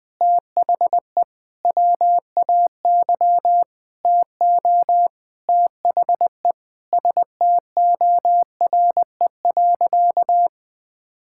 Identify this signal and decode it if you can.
Morse code, heard as a keyed tone, about 21 wpm, 700 Hz. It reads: THE WAY TO THE STORE.
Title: Sentences with 5-Word Phrases - 20wpm